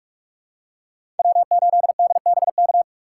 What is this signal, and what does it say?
Morse code, heard as a keyed tone, about 42 wpm, 700 Hz. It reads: W8DBK
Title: USA Call Signs - 5 Characters - 4x Character Spacing - 45wpm